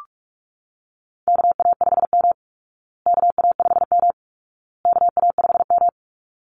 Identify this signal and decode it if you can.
Morse code, heard as a keyed tone, about 44 wpm, 700 Hz. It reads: KA5M KA5M KA5M
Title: Call Signs Repeated 2x - 45wpm